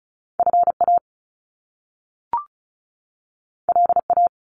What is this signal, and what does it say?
Morse code, heard as a keyed tone, about 36 wpm, 700 Hz. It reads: FA LA
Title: Top Two Letter Combinations in Words 35wpm